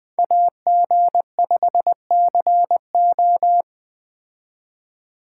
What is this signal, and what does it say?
Morse code, heard as a keyed tone, about 20 wpm, 700 Hz. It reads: AG5CO
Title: Speed Racing - USA 2x2 Call Signs 17wpm